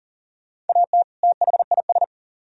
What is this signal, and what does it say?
Morse code, heard as a keyed tone, about 50 wpm, 700 Hz. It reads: AT THIS
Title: Speed-Racing - Sets of 2 Words 40wpm